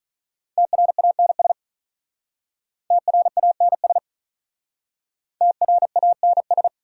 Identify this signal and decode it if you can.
Morse code, heard as a keyed tone, about 46 wpm, 700 Hz. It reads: TRANS TRANS TRANS